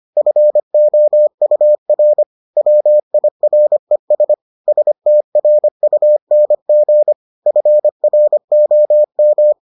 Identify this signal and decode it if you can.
Morse code, heard as a keyed tone, about 25 wpm, 600 Hz. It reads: FOUR WIRES STRUNG FROM